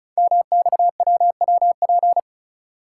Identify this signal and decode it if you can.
Morse code, heard as a keyed tone, about 34 wpm, 700 Hz. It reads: MXWWP
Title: Code Groups - 5 Letters - 35wpm